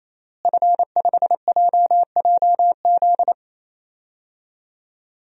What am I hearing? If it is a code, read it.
Morse code, heard as a keyed tone, about 27 wpm, 700 Hz. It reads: F5JJZ
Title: DXCC-100 Call Signs - 5 Characters - Repeated 3x - 28wpm